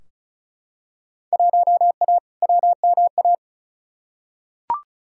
Morse code, heard as a keyed tone, about 36 wpm, 700 Hz. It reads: 1A WMA
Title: Field Day Exchanges - 35wpm